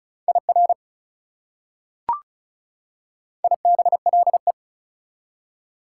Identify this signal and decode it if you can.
Morse code, heard as a keyed tone, about 36 wpm, 700 Hz. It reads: IR IBLE
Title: Prefixes and Suffixes 35wpm